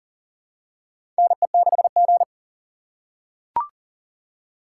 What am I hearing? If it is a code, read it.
Morse code, heard as a keyed tone, about 39 wpm, 700 Hz. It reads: NE6G
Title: USA Call Signs - 4 Characters - 6x Character Spacing - 40wpm